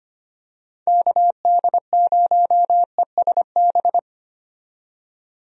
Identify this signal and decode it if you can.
Morse code, heard as a keyed tone, about 25 wpm, 700 Hz. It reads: KD0ESB
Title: USA Call Signs - 6 Characters - 25wpm